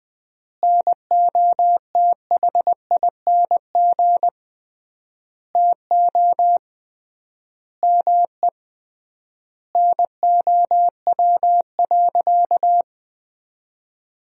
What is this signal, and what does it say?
Morse code, heard as a keyed tone, about 20 wpm, 700 Hz. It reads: NOTHING TO ME NOW.
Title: 3x Word-Spacing - Sentences from Top 500 Words - 20wpm